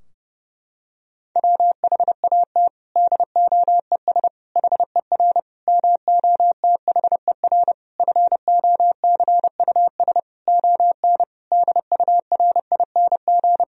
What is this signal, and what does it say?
Morse code, heard as a keyed tone, about 30 wpm, 700 Hz. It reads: WHAT DOES HER MOTHER FOCUS ON DURING
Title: Sentences from Top 1000 Words - Review Part 4 - 30wpm